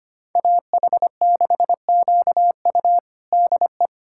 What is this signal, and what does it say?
Morse code, heard as a keyed tone, about 25 wpm, 700 Hz. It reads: AH6QU DE
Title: USA Callsign DE Callsign - 5 Characters - 25wpm